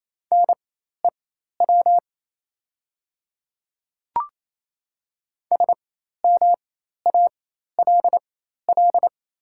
Morse code, heard as a keyed tone, about 28 wpm, spaced as for 13 wpm, 700 Hz. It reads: NEW SMALL
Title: 4x Character Spacing - Top 300 Words - 28wpm